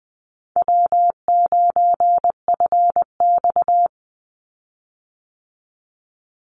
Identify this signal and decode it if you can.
Morse code, heard as a keyed tone, about 20 wpm, 700 Hz. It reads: W9FX
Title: Easy Call Signs Repeated 3x - 20wpm